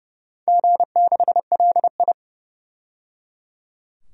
Morse code, heard as a keyed tone, about 30 wpm, 700 Hz. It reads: G6LI
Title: DXCC-100 Call Signs - 4 Characters - 30wpm